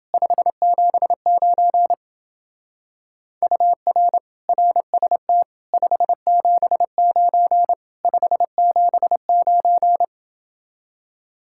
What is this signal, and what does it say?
Morse code, heard as a keyed tone, about 29 wpm, 700 Hz. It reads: 579 UR RST 579 579